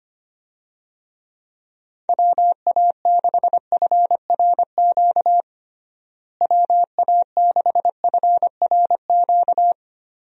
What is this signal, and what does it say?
Morse code, heard as a keyed tone, about 25 wpm, 700 Hz. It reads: WA6FRQ WA6FRQ
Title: USA Call Signs - 6 Characters - Repeated 2x - 25wpm